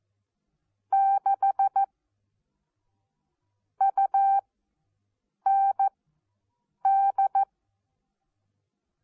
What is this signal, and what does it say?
Morse code, heard as a keyed tone, about 14 wpm, spaced as for 7 wpm, 780 Hz. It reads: 6 UND
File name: Onlinelehrgang / morsen / DJ4UF Morsen14